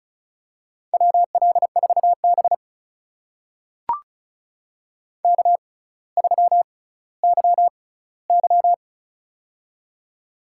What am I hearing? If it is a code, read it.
Morse code, heard as a keyed tone, about 36 wpm, spaced as for 18 wpm, 700 Hz. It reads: WL4B K3YY
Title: USA Call Signs - 4 Characters - 6x Character Spacing - 35wpm